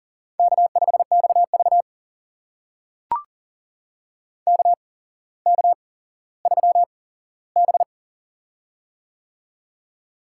Morse code, heard as a keyed tone, about 40 wpm, spaced as for 17 wpm, 700 Hz. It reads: K5XV KK3B
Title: USA Call Signs - 4 Characters - 8x Character Spacing - 40wpm